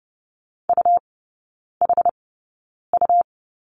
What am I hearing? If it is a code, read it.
Morse code, heard as a keyed tone, about 30 wpm, spaced as for 9 wpm, 700 Hz. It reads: UHU
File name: Lesson 17e - Letter u - Three Characters 30wpm